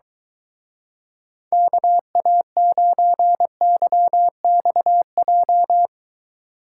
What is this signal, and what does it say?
Morse code, heard as a keyed tone, about 23 wpm, 700 Hz. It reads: KA9YXJ